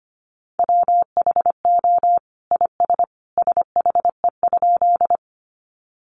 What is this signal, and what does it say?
Morse code, heard as a keyed tone, about 25 wpm, 700 Hz. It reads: WHO IS SHE?
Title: Sentences from Top 100 Words 25wpm